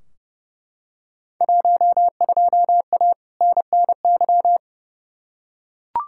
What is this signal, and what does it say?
Morse code, heard as a keyed tone, about 30 wpm, 700 Hz. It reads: 12A NNY
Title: Field Day Exchanges Encore - 30wpm